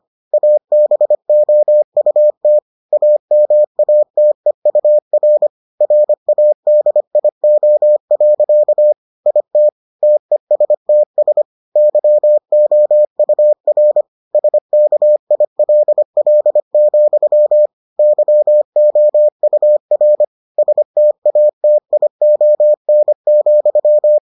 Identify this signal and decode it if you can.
Morse code, heard as a keyed tone, about 25 wpm, 600 Hz. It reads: ABOUT AMATEUR RADIO. IT TESTS YOUR SKILL, YOUR STATION,